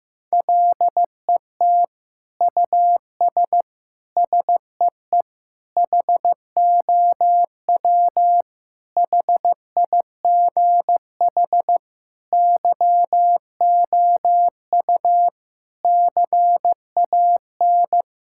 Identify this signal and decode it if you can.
Morse code, heard as a keyed tone, about 15 wpm, 700 Hz. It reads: LET US SEE HOW HIGH YOU CAN
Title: Sentences from Top 1000 Words - Review Part 6 - 15wpm